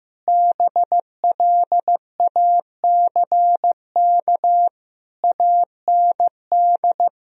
Morse code, heard as a keyed tone, about 16 wpm, 700 Hz. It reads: BLACK AND
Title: Sentences from Top 1000 Words - Review Part 2 - 15wpm